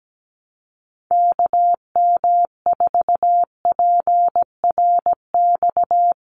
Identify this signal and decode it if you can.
Morse code, heard as a keyed tone, about 17 wpm, 700 Hz. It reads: KM4PRX
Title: USA Call Signs 17wpm